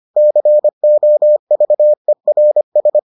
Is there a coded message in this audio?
Yes: Morse code from 0.2 to 3.0 s, heard as a keyed tone, about 25 wpm, 600 Hz. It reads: COVERS